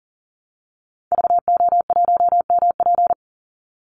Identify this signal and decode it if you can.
Morse code, heard as a keyed tone, about 40 wpm, 700 Hz. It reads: VO1MP
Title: Call Signs Repeated 3x - 40wpm